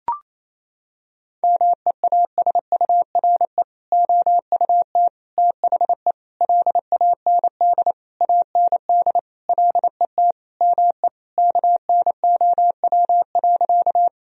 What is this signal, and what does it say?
Morse code, heard as a keyed tone, about 28 wpm, 700 Hz. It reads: MEASURE OUT THE LAND AND LET ME KNOW.
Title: Sentences from Top 1000 Words - Review Part 2 - 28wpm